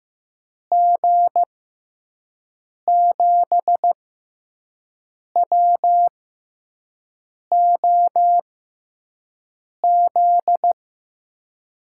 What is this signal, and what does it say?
Morse code, heard as a keyed tone, about 15 wpm, spaced as for 5 wpm, 700 Hz. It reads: G7WOZ
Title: USA Call Signs - 6 Characters - 6x Character Spacing - 15wpm